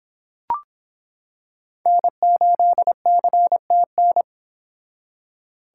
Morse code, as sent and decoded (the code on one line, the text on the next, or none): -. ---.. -.-. - -.
N8CTN